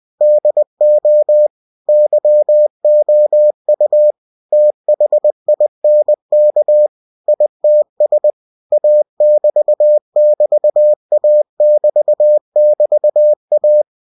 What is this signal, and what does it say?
Morse code, heard as a keyed tone, about 20 wpm, 600 Hz. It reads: DO YOU THINK ITS A<BT><BT>A<BT><BT>A
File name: winnie-20wpm